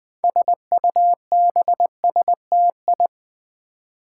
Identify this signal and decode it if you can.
Morse code, heard as a keyed tone, about 20 wpm, 700 Hz. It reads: SUBSTI
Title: Build A Word - 6 Letters At A Time - 12-Letter Words - 20wpm